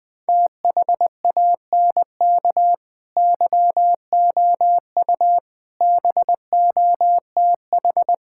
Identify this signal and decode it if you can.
Morse code, heard as a keyed tone, about 20 wpm, 700 Hz. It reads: THANK YOU BOTH
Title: Sentences with 5-Word Phrases - Spoken Beginning - 20wpm